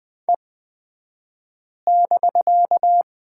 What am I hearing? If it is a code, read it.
Morse code, heard as a keyed tone, about 20 wpm, spaced as for 5 wpm, 700 Hz. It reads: E<BK>
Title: 249 Lesson 40d - BK - Two Characters 20wpm